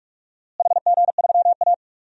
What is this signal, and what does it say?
Morse code, heard as a keyed tone, about 46 wpm, 700 Hz. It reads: HG3A
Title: DXCC-100 Call Signs - 4 Characters - Repeated 2x - 45wpm